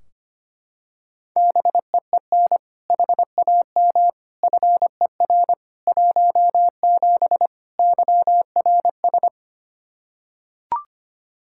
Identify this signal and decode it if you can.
Morse code, heard as a keyed tone, about 25 wpm, 700 Hz. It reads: BEEN HAM FER 17 YRS
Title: K1USN SST QSO Practice 25wpm